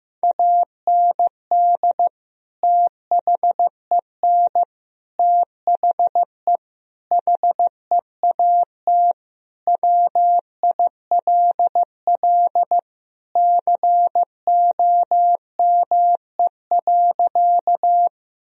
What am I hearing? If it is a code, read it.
Morse code, heard as a keyed tone, about 15 wpm, 700 Hz. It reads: AND THEN THE HEAT WILL COME.
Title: Sentences from Top 1000 Words - Review Part 3 - 15wpm